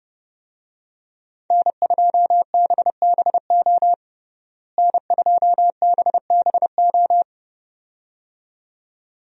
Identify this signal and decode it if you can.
Morse code, heard as a keyed tone, about 31 wpm, 700 Hz. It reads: N2BBO N2BBO
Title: USA Call Signs - 5 Characters - Repeated 2x - 30wpm